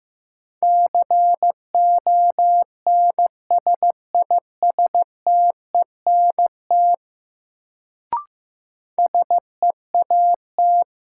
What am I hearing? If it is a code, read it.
Morse code, heard as a keyed tone, about 15 wpm, 700 Hz. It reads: CONSISTENT SEAT
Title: Sentences from Top 1500 Words - New Word Drill - 15wpm